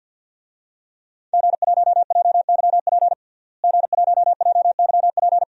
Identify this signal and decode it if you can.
Morse code, heard as a keyed tone, about 47 wpm, 700 Hz. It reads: G1JYP G1JYP
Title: DXCC-100 Call Signs - 5 Characters - Repeated 3x - 50wpm